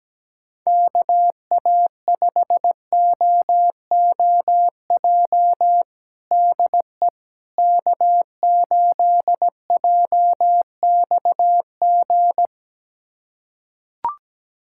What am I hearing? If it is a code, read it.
Morse code, heard as a keyed tone, about 17 wpm, 700 Hz. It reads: KA5OOJ DE K8JXG
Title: USA Callsign DE Callsign 17wpm